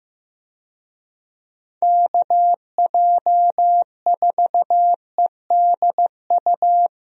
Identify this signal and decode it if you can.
Morse code, heard as a keyed tone, about 15 wpm, 700 Hz. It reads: KJ4EDU
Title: USA Call Signs - 6 Characters - Repeated 3x - 15wpm